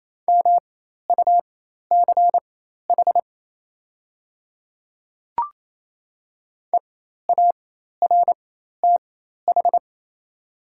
Morse code, heard as a keyed tone, about 28 wpm, spaced as for 13 wpm, 700 Hz. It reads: MUCH EARTH
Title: 4x Character Spacing - Top 200 Words - 28wpm